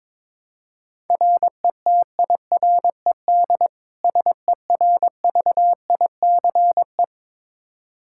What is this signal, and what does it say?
Morse code, heard as a keyed tone, about 22 wpm, 700 Hz. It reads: RETIRED SERVICE
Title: Practice QSO - Exchange 3 - More Info - 22wpm